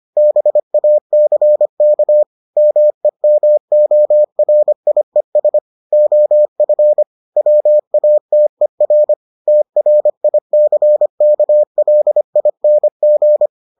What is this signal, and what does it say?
Morse code, heard as a keyed tone, about 25 wpm, 600 Hz. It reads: BACK MEMORIES OF WATER TRICKLING